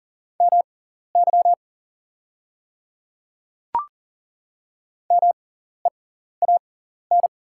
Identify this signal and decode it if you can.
Morse code, heard as a keyed tone, about 39 wpm, spaced as for 14 wpm, 700 Hz. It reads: MY MEAN
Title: 6x Character Spacing - Top 200 Words - 40wpm